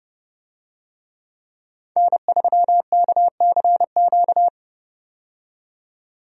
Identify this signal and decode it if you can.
Morse code, heard as a keyed tone, about 31 wpm, 700 Hz. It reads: N3KCQ
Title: USA Call Signs - 5 Characters - 30wpm